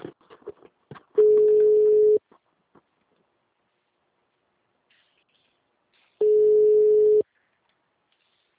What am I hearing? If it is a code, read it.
Ringback tone (Europe, 425 Hz).